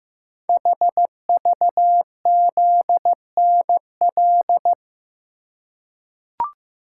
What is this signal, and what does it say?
Morse code, heard as a keyed tone, about 15 wpm, 700 Hz. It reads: HVZNL
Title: Code Groups - 5 Letters - 15wpm